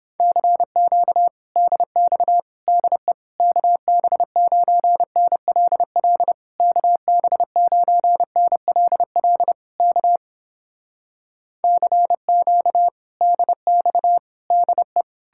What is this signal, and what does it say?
Morse code, heard as a keyed tone, about 29 wpm, 700 Hz. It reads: CQ DX DE KB9NLL KB9NLL K CQ DX DE